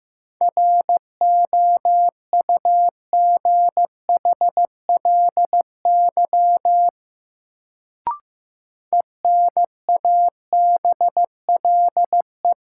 Morse code, heard as a keyed tone, about 15 wpm, 700 Hz. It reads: ROUGHLY ENABLE
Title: Sentences from Top 1500 Words - New Word Drill Alt - 15wpm